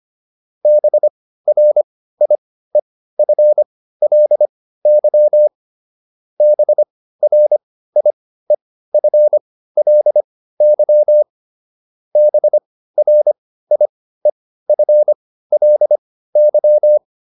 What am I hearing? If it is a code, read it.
Morse code, heard as a keyed tone, about 25 wpm, spaced as for 15 wpm, 600 Hz. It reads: BRIEFLY BRIEFLY BRIEFLY